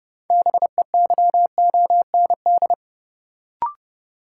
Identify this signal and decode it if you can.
Morse code, heard as a keyed tone, about 30 wpm, 700 Hz. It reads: BEYOND
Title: Sentences from Top 1000 Words - Review Part 6 - New Word Drill - 30wpm